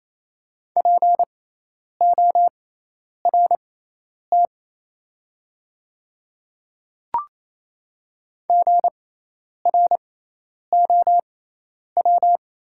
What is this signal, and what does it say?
Morse code, heard as a keyed tone, about 28 wpm, spaced as for 10 wpm, 700 Hz. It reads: PORT GROW
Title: 6x Character Spacing - Top 300 Words - 28wpm